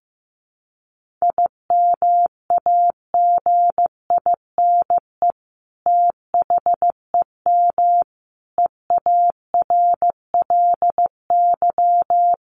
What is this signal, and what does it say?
Morse code, heard as a keyed tone, about 15 wpm, 700 Hz. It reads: IMAGINE THEM EARLY
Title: Sentences from Top 1400 Words 15wpm